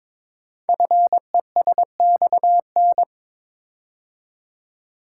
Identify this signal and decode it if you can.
Morse code, heard as a keyed tone, about 22 wpm, 700 Hz. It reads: FESXN
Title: Code Groups - 5 Letters - 22wpm